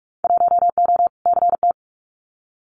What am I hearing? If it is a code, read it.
Morse code, heard as a keyed tone, about 46 wpm, 700 Hz. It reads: 1O CT
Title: Winter Field Day Exchanges 45wpm